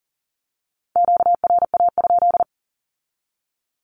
Morse code, heard as a keyed tone, about 38 wpm, 700 Hz. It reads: QRA?
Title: Q-Codes with Spoken Meaning - 40wpm